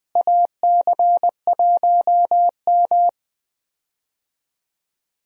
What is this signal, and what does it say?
Morse code, heard as a keyed tone, about 21 wpm, 700 Hz. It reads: AC1M
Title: USA Call Signs - 4 Characters - Repeated 3x - 20wpm